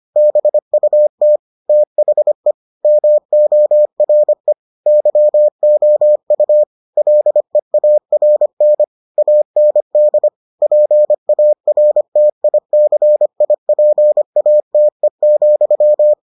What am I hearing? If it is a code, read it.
Morse code, heard as a keyed tone, about 25 wpm, 600 Hz. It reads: BUT THE MORE YOU LEARN AND PARTICIPATE,